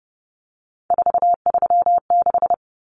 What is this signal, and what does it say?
Morse code, heard as a keyed tone, about 30 wpm, 700 Hz. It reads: 436
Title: Numbers - 3-Digit 30wpm